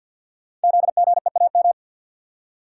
Morse code, heard as a keyed tone, about 52 wpm, 700 Hz. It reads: ZGEAM
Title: Code Groups - 5 Letters - 50wpm